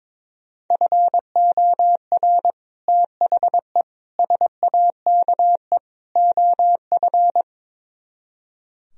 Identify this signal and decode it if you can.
Morse code, heard as a keyed tone, about 22 wpm, 700 Hz. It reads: FOR THE SAKE OF
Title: Speed-Racing - Sets of 4 Words 22wpm